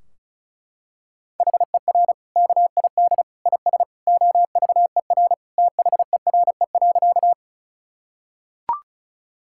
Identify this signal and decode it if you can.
Morse code, heard as a keyed tone, about 36 wpm, 700 Hz. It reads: HER KID IS OVER THERE.
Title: Sentences from Top 1600 Words 35wpm